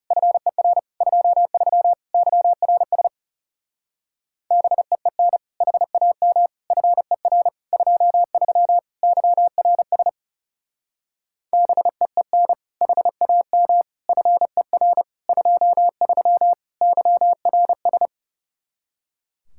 Morse code, heard as a keyed tone, about 39 wpm, 700 Hz. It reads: FER 23 YRS BEEN HAM FER 23 YRS BEEN HAM FER 23 YRS